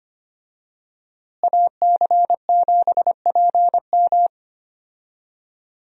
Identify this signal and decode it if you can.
Morse code, heard as a keyed tone, about 25 wpm, 700 Hz. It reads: AC7PM
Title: USA Call Signs - 5 Characters - Repeated 2x - 25wpm